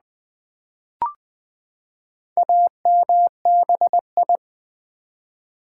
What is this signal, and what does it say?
Morse code, heard as a keyed tone, about 20 wpm, 700 Hz. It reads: AMBI
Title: Build A Word - 4 Letters At A Time - 8-Letter Words - 20wpm